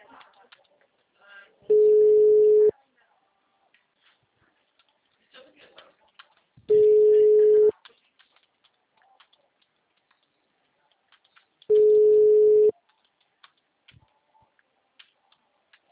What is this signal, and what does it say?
Ringback tone (Europe, 425 Hz), with speech.